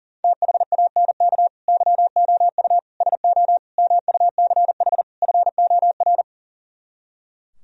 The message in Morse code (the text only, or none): THANK YOU SO MUCH FOR